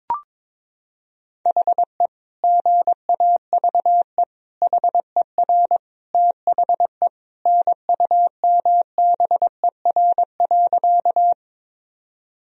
Morse code, heard as a keyed tone, about 22 wpm, 700 Hz. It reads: HE GAVE HER THE NUMBER.
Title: Sentences from Top 1000 Words - Review Part 1 - 22wpm